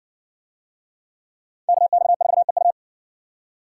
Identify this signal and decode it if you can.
Morse code, heard as a keyed tone, about 58 wpm, 700 Hz. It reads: KX4EU